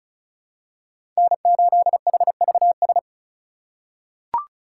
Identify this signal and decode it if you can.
Morse code, heard as a keyed tone, about 36 wpm, 700 Hz. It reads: N8HVS
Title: USA Call Signs - 5 Characters - 2x Character Spacing - 35wpm